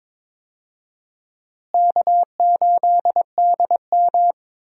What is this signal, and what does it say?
Morse code, heard as a keyed tone, about 22 wpm, 700 Hz. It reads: K8DM
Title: USA Call Signs - 4 Characters - Repeated 2x - 22wpm